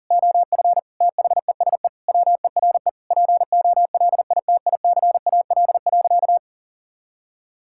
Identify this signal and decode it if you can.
Morse code, heard as a keyed tone, about 40 wpm, 700 Hz. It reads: OF THESE WERE POLITICAL.
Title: Sentences from Top 1000 Words - Review Part 5 - 40wpm